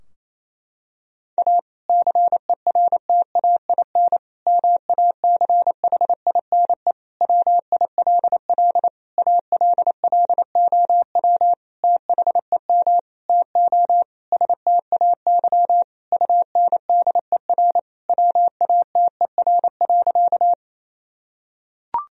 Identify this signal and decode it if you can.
Morse code, heard as a keyed tone, about 27 wpm, 700 Hz. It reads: A CERTAIN MACHINE WILL ALLOW THEM TO STAY UNDER WATER.
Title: Sentences from Top 1000 Words - Review Part 2 - 28wpm